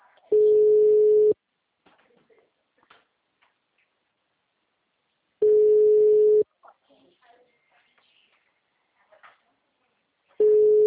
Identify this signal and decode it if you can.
Ringback tone (Europe, 425 Hz), with speech.